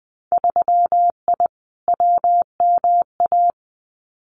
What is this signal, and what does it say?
Morse code, heard as a keyed tone, about 20 wpm, 700 Hz. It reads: 3I WMA
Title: Winter Field Day Exchanges Encore 20wpm